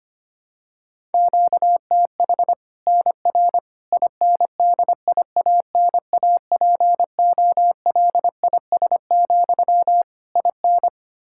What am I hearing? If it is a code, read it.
Morse code, heard as a keyed tone, about 25 wpm, 700 Hz. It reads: QTH NR INDIANAPOLIS, IN
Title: Practice QSO - Exchange 1 - Beginning - 25wpm